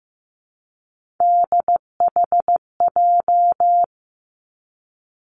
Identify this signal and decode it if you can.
Morse code, heard as a keyed tone, about 15 wpm, 700 Hz. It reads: DHJ
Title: Three Challenging Letter-Number Combinations 15wpm